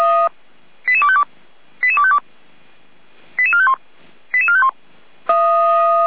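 Five-tone selective call (ZVEI), heard as quick runs of five tones, tones of about 70 ms. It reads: 88252 88252 88451 88451
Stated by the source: ZVEI five-tone selective call received by SDR, from the Feuerwehr Orlamünde audio server.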